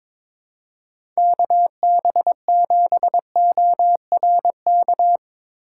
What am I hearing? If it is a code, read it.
Morse code, heard as a keyed tone, about 24 wpm, 700 Hz. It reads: KB7ORK
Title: Speed Racing - USA 2x3 Call Signs 22wpm